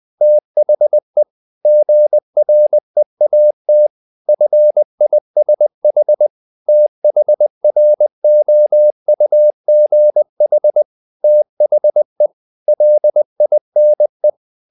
Morse code, heard as a keyed tone, about 20 wpm, 600 Hz. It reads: THE GREAT FISH THROUGH THE LINE
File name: the old man and the sea / the old man and the sea part 1 20wpm